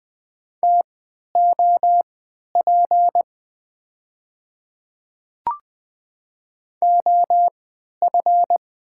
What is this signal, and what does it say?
Morse code, heard as a keyed tone, about 20 wpm, spaced as for 11 wpm, 700 Hz. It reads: TOP OF
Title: 3x Character Spacing - Top 500 Words - 20wpm